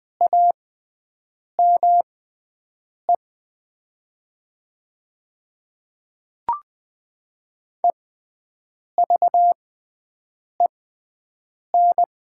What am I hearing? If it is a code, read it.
Morse code, heard as a keyed tone, about 21 wpm, spaced as for 7 wpm, 700 Hz. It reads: AME EVEN
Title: 6x Character Spacing - Top 300 Words - 20wpm